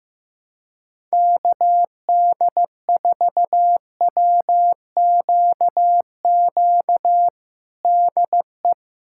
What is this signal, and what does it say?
Morse code, heard as a keyed tone, about 15 wpm, 700 Hz. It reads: KD4WQQ DE
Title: USA Callsign DE Callsign - 6 Characters - 15wpm